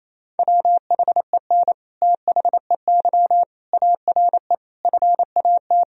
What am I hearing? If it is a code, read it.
Morse code, heard as a keyed tone, about 28 wpm, 700 Hz. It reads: WHEN THEY ARE FAT
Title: Sentences from Top 1000 Words - Review Part 4 - 28wpm